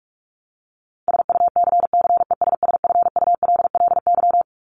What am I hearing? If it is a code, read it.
Morse code, heard as a keyed tone, about 44 wpm, 700 Hz. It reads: SUCCESSFULLY
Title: LY Suffix - Top 100 Words with Base Words 45wpm